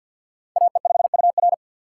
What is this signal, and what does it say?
Morse code, heard as a keyed tone, about 46 wpm, 700 Hz. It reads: AE5UR
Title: USA Call Signs - 5 Characters - 3x Character Spacing - 50wpm